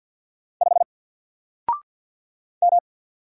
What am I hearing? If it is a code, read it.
Morse code, heard as a keyed tone, about 50 wpm, 700 Hz. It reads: 5 M
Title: Single Character (Letter:Number:Punctuation:Prosign) 50wpm